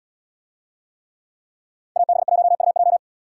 Morse code, heard as a keyed tone, about 77 wpm, 700 Hz. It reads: AB1UJ